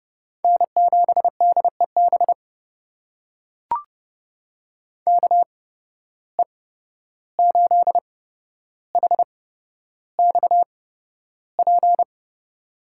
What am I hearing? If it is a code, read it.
Morse code, heard as a keyed tone, about 30 wpm, spaced as for 12 wpm, 700 Hz. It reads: N7DEB KE8HXP
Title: USA Call Signs - 8x Character Spacing - 30wpm